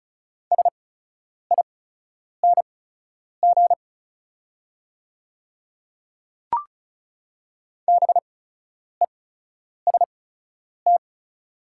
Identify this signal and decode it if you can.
Morse code, heard as a keyed tone, about 36 wpm, spaced as for 10 wpm, 690 Hz. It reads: SING BEST
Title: 8x Character Spacing - Top 500 Words - 35wpm